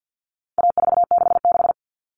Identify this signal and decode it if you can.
Morse code, heard as a keyed tone, about 50 wpm, 700 Hz. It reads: A466
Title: Common Cut Numbers - 4-Digit 50wpm